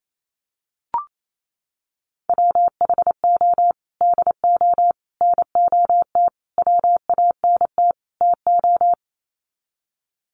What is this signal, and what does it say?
Morse code, heard as a keyed tone, about 28 wpm, 700 Hz. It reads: WHO DO NOT WANT TO
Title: Sets of 5 Words Encore2 28wpm